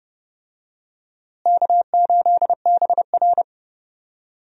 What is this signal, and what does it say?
Morse code, heard as a keyed tone, about 30 wpm, 700 Hz. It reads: K8BR
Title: USA Call Signs - 4 Characters - Repeated 3x - 30wpm